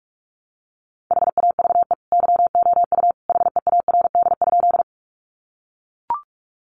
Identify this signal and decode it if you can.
Morse code, heard as a keyed tone, about 44 wpm, 700 Hz. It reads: HAVE YOU HEARD?
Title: Sentences from Top 400 Words 45wpm